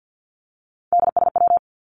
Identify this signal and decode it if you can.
Morse code, heard as a keyed tone, about 49 wpm, 700 Hz. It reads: DSW
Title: Abbreviations 50wpm